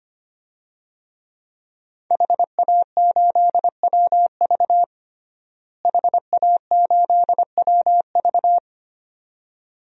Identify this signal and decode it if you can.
Morse code, heard as a keyed tone, about 25 wpm, 700 Hz. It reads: HA8WV HA8WV
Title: DXCC-100 Call Signs - 5 Characters - Repeated 2x - 25wpm